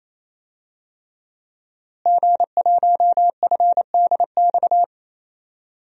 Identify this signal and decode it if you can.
Morse code, heard as a keyed tone, about 28 wpm, 700 Hz. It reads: G1FDX
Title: DXCC-100 Call Signs - 5 Characters - Repeated 3x - 28wpm